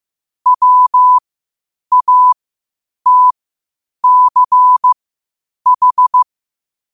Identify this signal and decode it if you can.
Morse code, heard as a keyed tone, about 15 wpm, spaced as for 8 wpm, 1000 Hz. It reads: WATCH
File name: qotdfiles / 8wpm